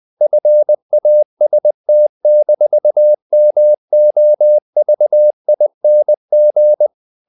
Morse code, heard as a keyed tone, about 20 wpm, 600 Hz. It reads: FAST-MOVING
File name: the old man and the sea part 2 20wpm